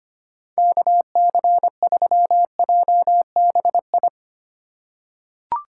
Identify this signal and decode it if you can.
Morse code, heard as a keyed tone, about 25 wpm, 700 Hz. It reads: KC3JBI
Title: USA Call Signs - 6 Characters - 3x Character Spacing - 25wpm